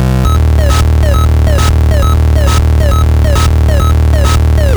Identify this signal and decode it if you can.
There is no code.